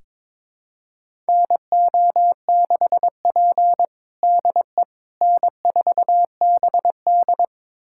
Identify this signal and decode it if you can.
Morse code, heard as a keyed tone, about 21 wpm, 700 Hz. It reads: NO6P DE N4BD
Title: USA Callsign DE Callsign - 4 Characters - 22wpm